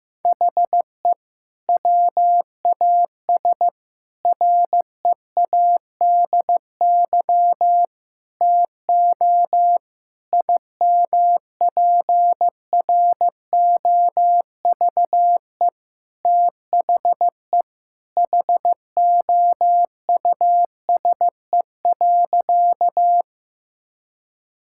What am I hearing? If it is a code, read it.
Morse code, heard as a keyed tone, about 15 wpm, 700 Hz. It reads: HE WAS READY TO IMPROVE THE HOUSE.
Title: Sentences from Top 1000 Words - Review Part 5 - 15wpm